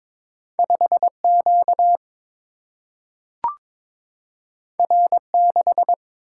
Morse code, heard as a keyed tone, about 22 wpm, 700 Hz. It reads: 5 Q R 6
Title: Single-Letter Single-Number 22wpm